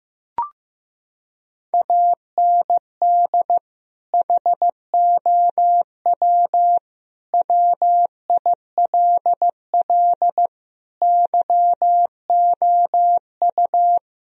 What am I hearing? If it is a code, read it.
Morse code, heard as a keyed tone, about 16 wpm, 700 Hz. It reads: AND HOW WILL YOU
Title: Sentences from Top 1000 Words - Review Part 4 - 15wpm